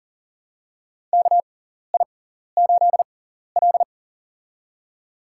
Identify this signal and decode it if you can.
Morse code, heard as a keyed tone, about 40 wpm, spaced as for 14 wpm, 700 Hz. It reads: KI8L
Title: USA Call Signs - 6x Character Spacing - 40wpm